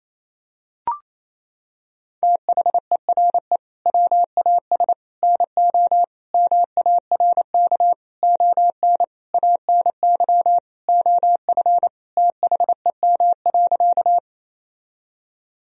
Morse code, heard as a keyed tone, about 28 wpm, 700 Hz. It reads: THERE WAS NO MARK ON ANY OF THEM.
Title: Sentences from Top 1000 Words - Review Part 1 - 28wpm